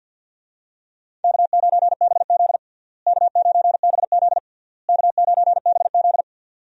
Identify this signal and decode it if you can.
Morse code, heard as a keyed tone, about 47 wpm, 700 Hz. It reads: K9BZ K9BZ K9BZ
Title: USA Call Signs - 4 Characters - Repeated 3x - 50wpm